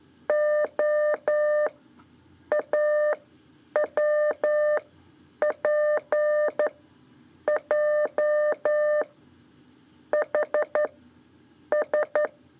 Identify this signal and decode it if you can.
Morse code, heard as a keyed tone, about 11 wpm, 600 Hz. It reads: O A W P J H S
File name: 3 lezione CW 2 AIR